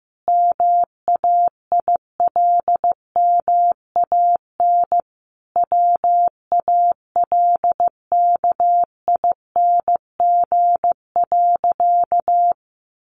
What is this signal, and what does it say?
Morse code, heard as a keyed tone, about 15 wpm, 700 Hz. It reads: MAILMAN WALKING.
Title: Additional Easy Phrases 15wpm